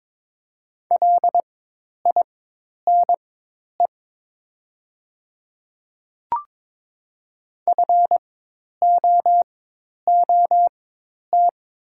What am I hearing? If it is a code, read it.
Morse code, heard as a keyed tone, about 22 wpm, spaced as for 10 wpm, 700 Hz. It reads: LINE FOOT
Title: 4x Character Spacing - Top 500 Words - 22wpm